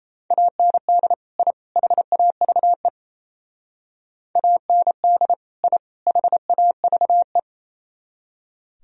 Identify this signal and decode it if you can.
Morse code, heard as a keyed tone, about 36 wpm, 700 Hz. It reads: AND I HAVE AND I HAVE